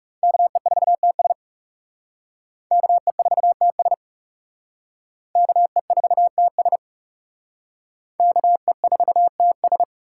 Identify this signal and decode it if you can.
Morse code, heard as a keyed tone, about 37 wpm, 700 Hz. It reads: KE4TS KE4TS KE4TS KE4TS
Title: Speed Racing - USA 2x2 Call Signs 30wpm